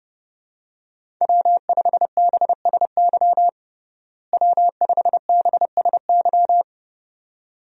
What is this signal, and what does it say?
Morse code, heard as a keyed tone, about 30 wpm, 700 Hz. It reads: W5BSY W5BSY
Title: USA Call Signs - 5 Characters - Repeated 2x - 30wpm